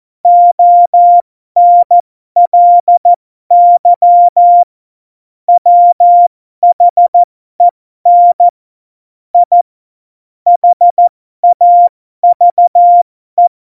Morse code, heard as a keyed tone, about 14 wpm, spaced as for 12 wpm, 700 Hz. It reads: ONLY WHEN I HAVE
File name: Morsekurs / Witze